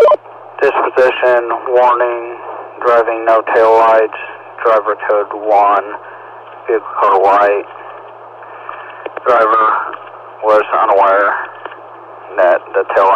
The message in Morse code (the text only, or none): none